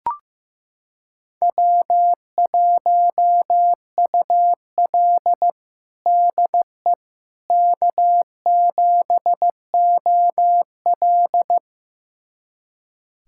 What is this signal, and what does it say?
Morse code, heard as a keyed tone, about 15 wpm, 700 Hz. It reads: W1UL DE K7OL
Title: USA Callsign DE Callsign - 4 Characters - 15wpm